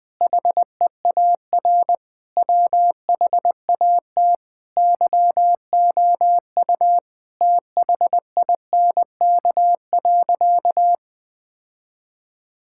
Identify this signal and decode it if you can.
Morse code, heard as a keyed tone, about 20 wpm, 700 Hz. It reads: HEAR WHAT YOU THINK.
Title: Sentences with 6-Word Phrases - 20wpm